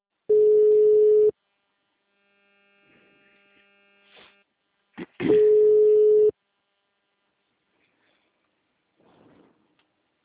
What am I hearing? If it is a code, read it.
Ringback tone (Europe, 425 Hz).
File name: out- -2016- - - .877